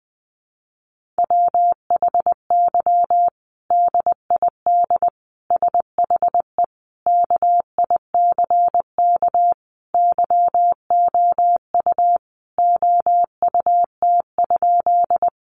Morse code, heard as a keyed tone, about 20 wpm, 700 Hz. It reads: WHY DID SHE KICK YOU OUT?
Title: Sentences from Top 1300 Words 20wpm